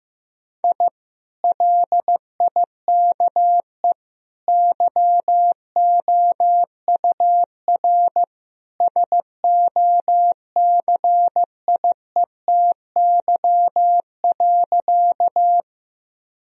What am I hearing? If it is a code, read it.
Morse code, heard as a keyed tone, about 14 wpm, 700 Hz. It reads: I LIKE YOUR SOCIETY.
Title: Sentences from Top 1000 Words - Review Part 4 - 15wpm